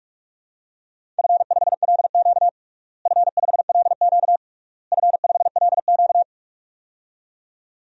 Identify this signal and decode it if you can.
Morse code, heard as a keyed tone, about 41 wpm, 700 Hz. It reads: F5LQ F5LQ F5LQ
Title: DXCC-100 Call Signs - 4 Characters - Repeated 3x - 45wpm